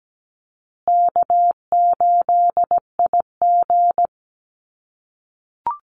Morse code, heard as a keyed tone, about 17 wpm, 700 Hz. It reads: K8IG
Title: USA Call Signs Repeated 3x 17wpm